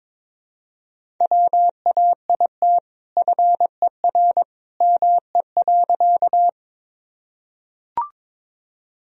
Morse code, heard as a keyed tone, about 22 wpm, 700 Hz. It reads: WAIT FER ME.
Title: Sentences from Top 500 Words and Abbreviations 22wpm